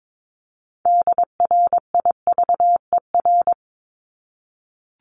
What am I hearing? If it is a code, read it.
Morse code, heard as a keyed tone, about 22 wpm, 700 Hz. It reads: DRIVER
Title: Top 751-1000 Words 22wpm